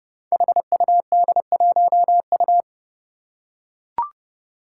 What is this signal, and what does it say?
Morse code, heard as a keyed tone, about 30 wpm, 700 Hz. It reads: HUD1U
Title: Five Challenging Letter-Number Combinations 30wpm